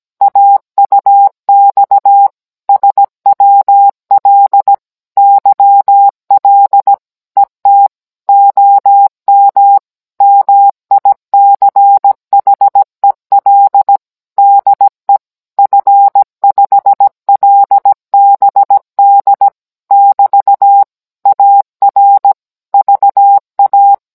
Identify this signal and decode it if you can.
Morse code, heard as a keyed tone, about 17 wpm, 800 Hz. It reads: AUX SWL YL ET OM MICHEL DE F5LBD <BT> AR VA